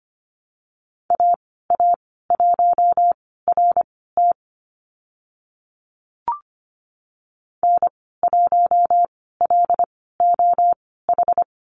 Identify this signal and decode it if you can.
Morse code, heard as a keyed tone, about 25 wpm, spaced as for 16 wpm, 700 Hz. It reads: AA1RT N1LOH
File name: Lesson 15g - Number 5 - Call Signs 25wpm